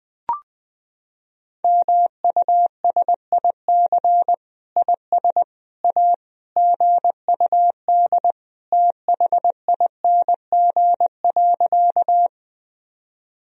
Morse code, heard as a keyed tone, about 20 wpm, 700 Hz. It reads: MUSIC IS A GUD THING.
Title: Sentences from Top 300 Words and Abbreviations 20wpm